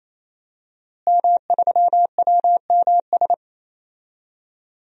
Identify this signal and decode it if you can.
Morse code, heard as a keyed tone, about 28 wpm, 700 Hz. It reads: M3WMS
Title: DXCC-100 Call Signs - 5 Characters - Repeated 3x - 28wpm